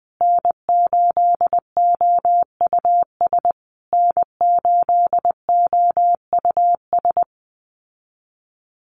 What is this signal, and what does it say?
Morse code, heard as a keyed tone, about 20 wpm, 700 Hz. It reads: N8OUS N8OUS
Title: USA Call Signs Repeated 3x 20wpm